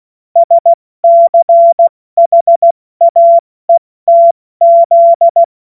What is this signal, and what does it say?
Morse code, heard as a keyed tone, about 16 wpm, 680 Hz. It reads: SCHAETZ
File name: Morsekurs / Witze / Witz 14. September 2021, 70 F80